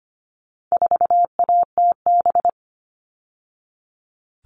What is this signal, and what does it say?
Morse code, heard as a keyed tone, about 25 wpm, 700 Hz. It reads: 4ATB
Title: All Cut Numbers - 4-Digit 25wpm